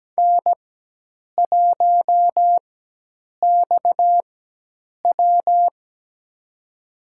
Morse code, heard as a keyed tone, about 17 wpm, spaced as for 8 wpm, 700 Hz. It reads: N1XW
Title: USA Call Signs - 4 Characters - 4x Character Spacing - 17wpm